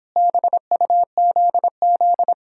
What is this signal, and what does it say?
Morse code, heard as a keyed tone, about 29 wpm, 700 Hz. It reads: BUZZ